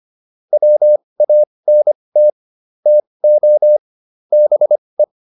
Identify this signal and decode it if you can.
Morse code, heard as a keyed tone, about 25 wpm, spaced as for 20 wpm, 600 Hz. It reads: WANT TO BE